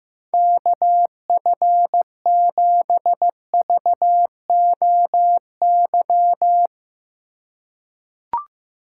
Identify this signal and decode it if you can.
Morse code, heard as a keyed tone, about 15 wpm, 700 Hz. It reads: KF7VOY
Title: USA Call Signs - 6 Characters - 4x Character Spacing - 15wpm